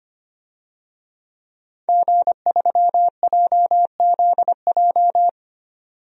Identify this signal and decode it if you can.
Morse code, heard as a keyed tone, about 25 wpm, 700 Hz. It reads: G3JZJ
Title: DXCC-100 Call Signs - 5 Characters - 25wpm